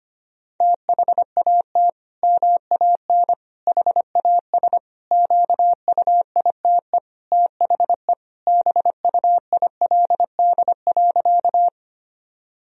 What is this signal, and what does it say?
Morse code, heard as a keyed tone, about 25 wpm, 700 Hz. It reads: THAT MAN HAS QUITE THE BUILD.
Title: Sentences from Top 1000 Words - Review Part 1 - 25wpm